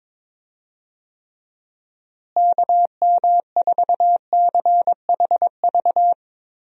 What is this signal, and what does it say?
Morse code, heard as a keyed tone, about 22 wpm, 700 Hz. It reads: KM4CHV
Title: USA Call Signs - 6 Characters - Repeated 2x - 22wpm